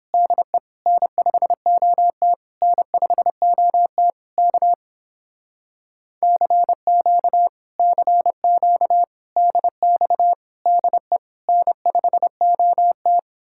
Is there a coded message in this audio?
Yes: Morse code from 0.1 to 13.2 s, heard as a keyed tone, about 31 wpm, 700 Hz. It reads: DE N5OT N5OT K CQ CQ DX DE N5OT